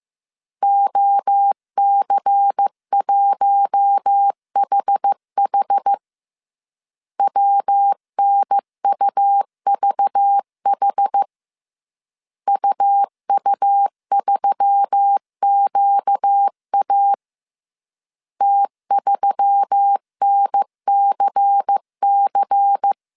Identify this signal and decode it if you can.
Morse code, heard as a keyed tone, about 15 wpm, 800 Hz. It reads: OC1HH WNUVH UU3QA T3NCC